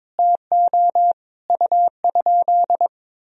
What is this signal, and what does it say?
Morse code, heard as a keyed tone, about 22 wpm, 700 Hz. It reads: TO U?
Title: Sentences from Top 500 Words and Abbreviations 22wpm